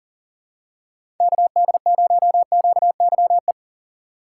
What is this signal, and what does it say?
Morse code, heard as a keyed tone, about 40 wpm, 700 Hz. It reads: KD0QYE
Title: USA Call Signs - 6 Characters - 2x Character Spacing - 40wpm